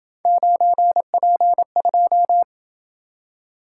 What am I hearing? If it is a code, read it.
Morse code, heard as a keyed tone, about 27 wpm, 700 Hz. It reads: 9P2